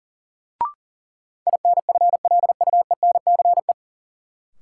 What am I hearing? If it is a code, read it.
Morse code, heard as a keyed tone, about 40 wpm, 700 Hz. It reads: INFLUENCE
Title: Sentences from Top 1100 Words - New Word Drill Alt - 40wpm